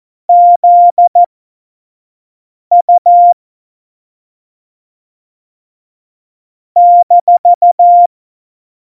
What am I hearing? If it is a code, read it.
Morse code, heard as a keyed tone, about 14 wpm, spaced as for 5 wpm, 700 Hz. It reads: ZU -
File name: Morsekurs / Witze / Witz 01. Dezember, 2020 25 F70